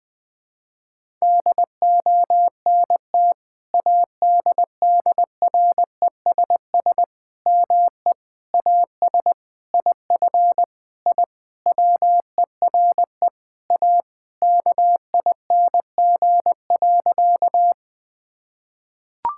Morse code, heard as a keyed tone, about 20 wpm, 700 Hz. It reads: DONT ADDRESS ME AS IF I WERE A KING.
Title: Sentences from Top 1000 Words - Review Part 6 - 20wpm